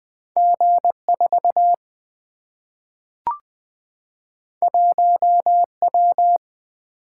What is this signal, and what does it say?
Morse code, heard as a keyed tone, about 20 wpm, 700 Hz. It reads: G4 1W
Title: Single-Letter Single-Number 20wpm